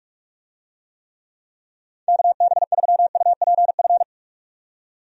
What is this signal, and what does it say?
Morse code, heard as a keyed tone, about 42 wpm, 700 Hz. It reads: KB3UPF
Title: USA Call Signs - 6 Characters - 45wpm